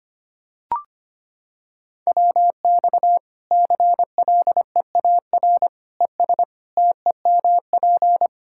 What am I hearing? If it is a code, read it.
Morse code, heard as a keyed tone, about 25 wpm, 700 Hz. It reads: WX CLEAR ES TEMP
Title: Practice QSO - Exchange 2 - Info - 25wpm